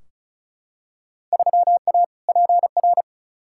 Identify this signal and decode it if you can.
Morse code, heard as a keyed tone, about 36 wpm, 700 Hz. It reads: 3A PR
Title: Field Day Exchanges Encore - 35wpm